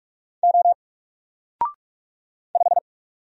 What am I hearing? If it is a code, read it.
Morse code, heard as a keyed tone, about 44 wpm, 700 Hz. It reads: O 5
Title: Single Letter-Number Alt 45wpm